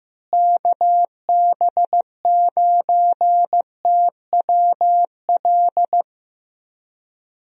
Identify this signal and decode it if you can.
Morse code, heard as a keyed tone, about 15 wpm, 700 Hz. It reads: KB9TWL
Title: USA Call Signs - 6 Characters - Repeated 2x - 15wpm